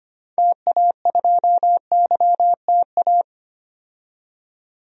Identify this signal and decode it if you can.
Morse code, heard as a keyed tone, about 25 wpm, 700 Hz. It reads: TA2YTA
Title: DXCC-100 Call Signs - 6 Characters - 25wpm